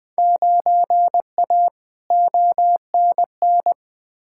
Morse code, heard as a keyed tone, about 20 wpm, 700 Hz. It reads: 9A ONN
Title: Field Day Exchanges Encore - 20wpm